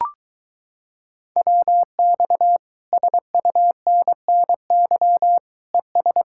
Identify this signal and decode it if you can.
Morse code, heard as a keyed tone, about 24 wpm, 700 Hz. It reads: WX SUNNY ES